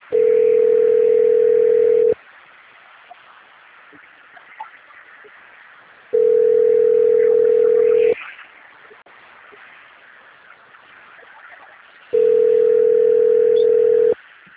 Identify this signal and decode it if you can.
Ringback tone (North America), 440 + 480 Hz.